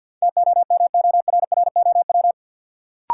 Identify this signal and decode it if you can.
Morse code, heard as a keyed tone, about 52 wpm, 700 Hz. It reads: TOMORROW
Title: Sentences from Top 1100 Words - New Word Drill Alt - 50wpm